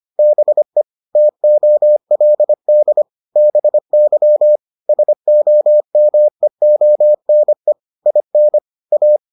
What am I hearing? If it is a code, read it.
Morse code, heard as a keyed tone, about 25 wpm, 600 Hz. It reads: BE TOLD BY SOMEONE IN A